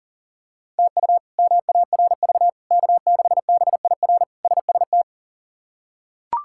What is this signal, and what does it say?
Morse code, heard as a keyed tone, about 40 wpm, 700 Hz. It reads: TU MARV K6BIR SST
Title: K1USN SST QSO Practice - 40wpm